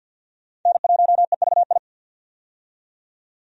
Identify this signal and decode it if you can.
Morse code, heard as a keyed tone, about 53 wpm, 700 Hz. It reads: N1EVI
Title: USA Call Signs - 5 Characters - 50wpm